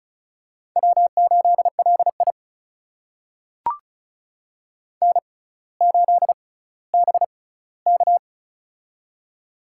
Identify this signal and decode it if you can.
Morse code, heard as a keyed tone, about 36 wpm, spaced as for 18 wpm, 700 Hz. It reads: W8LI N8BK
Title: USA Call Signs - 4 Characters - 6x Character Spacing - 35wpm